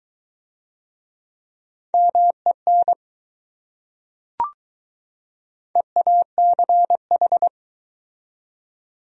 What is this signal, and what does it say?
Morse code, heard as a keyed tone, about 20 wpm, 700 Hz. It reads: MEN EACH